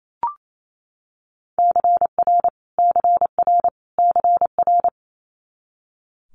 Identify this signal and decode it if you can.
Morse code, heard as a keyed tone, about 28 wpm, 700 Hz. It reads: CR CR CR
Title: Multi-letter Phonemes Repeated 3x 28wpm